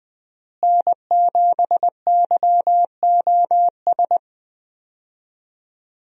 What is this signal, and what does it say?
Morse code, heard as a keyed tone, about 20 wpm, 700 Hz. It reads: N7YOS
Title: USA Call Signs - 5 Characters - Repeated 2x - 20wpm